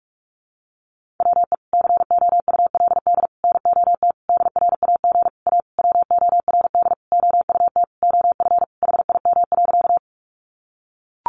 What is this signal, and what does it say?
Morse code, heard as a keyed tone, about 44 wpm, 700 Hz. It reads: WE COULD NOT DRAG A WORD OUT OF HIM.
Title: Sentences from Top 1500 Words 45wpm